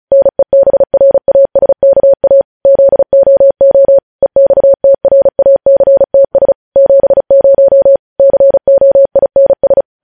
Morse code, heard as a keyed tone, about 36 wpm, 550 Hz. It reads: NEBRASKA ZOO EXTRACTS 70 COINS